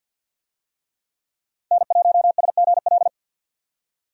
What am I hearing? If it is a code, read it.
Morse code, heard as a keyed tone, about 51 wpm, 700 Hz. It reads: N1SGL
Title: USA Call Signs - 5 Characters - 50wpm